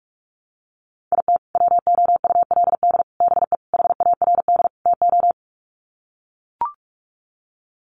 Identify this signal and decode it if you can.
Morse code, heard as a keyed tone, about 44 wpm, 700 Hz. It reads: IT WOULD BE HARD TO
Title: Sets of 5 Words Encore2 45wpm